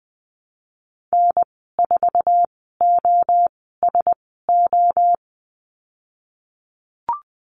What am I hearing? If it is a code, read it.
Morse code, heard as a keyed tone, about 20 wpm, spaced as for 14 wpm, 700 Hz. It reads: N4OSO
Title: 048 Lesson 9g - Number 4 - Call Signs 20wpm